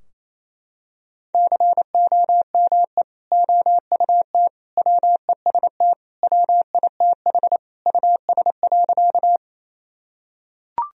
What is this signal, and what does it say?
Morse code, heard as a keyed tone, about 27 wpm, 700 Hz. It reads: COME OUT WEST WITH US.
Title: Sentences from Top 1000 Words - Review Part 2 - 28wpm